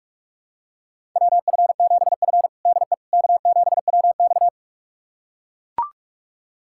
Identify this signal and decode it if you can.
Morse code, heard as a keyed tone, about 41 wpm, 700 Hz. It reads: WF7F DE K7WX
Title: USA Callsign DE Callsign - 4 Characters - 45wpm